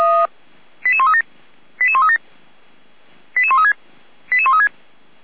Five-tone selective call (ZVEI), heard as quick runs of five tones, tones of about 70 ms. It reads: 88137 88137 88136 88136